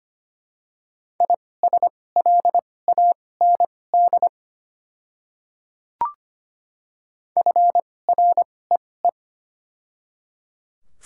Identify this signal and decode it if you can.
Morse code, heard as a keyed tone, about 25 wpm, spaced as for 18 wpm, 700 Hz. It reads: ISLAND FREE
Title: 2x Character Spacing - Top 500 Words - 25wpm